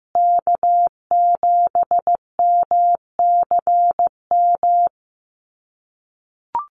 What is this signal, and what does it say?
Morse code, heard as a keyed tone, about 15 wpm, 700 Hz. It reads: K7MCM
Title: USA Call Signs Repeated 3x 15wpm